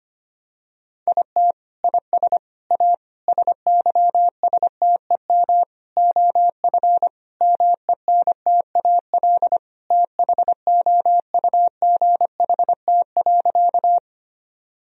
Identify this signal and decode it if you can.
Morse code, heard as a keyed tone, about 25 wpm, 700 Hz. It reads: IT IS A SYSTEM OF MENTAL THOUGHT.
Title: Sentences from Top 1000 Words - Review Part 4 - 25wpm